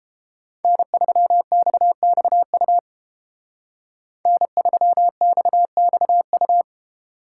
Morse code, heard as a keyed tone, about 31 wpm, 700 Hz. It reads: N3XXU N3XXU